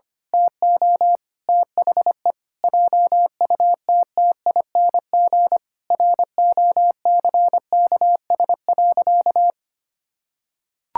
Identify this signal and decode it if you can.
Morse code, heard as a keyed tone, about 25 wpm, 700 Hz. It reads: TO THE JUTTING ROCKS.
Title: Pangram Challenge v2 - 25wpm